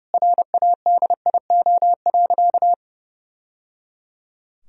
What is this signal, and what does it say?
Morse code, heard as a keyed tone, about 30 wpm, 700 Hz. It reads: RADIO.
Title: Sentences from Top 1000 Words - Review Part 3 - 30wpm